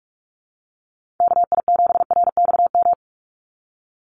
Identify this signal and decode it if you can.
Morse code, heard as a keyed tone, about 46 wpm, 690 Hz. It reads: KI7RXM
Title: USA Call Signs 45wpm